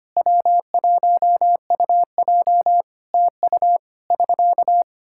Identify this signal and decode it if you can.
Morse code, heard as a keyed tone, about 25 wpm, 700 Hz. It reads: W1UJ TU <SK>
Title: Practice QSO - Exchange 4 - Ending - 25wpm